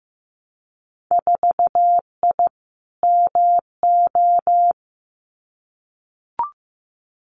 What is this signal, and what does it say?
Morse code, heard as a keyed tone, about 15 wpm, 700 Hz. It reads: 4I MO
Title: Winter Field Day Exchanges 15wpm